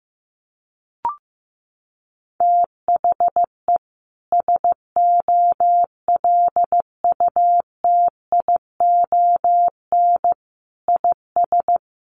Morse code, heard as a keyed tone, about 15 wpm, 700 Hz. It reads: THE SOLUTION IS
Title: Sentences from Top 1300 Words 15wpm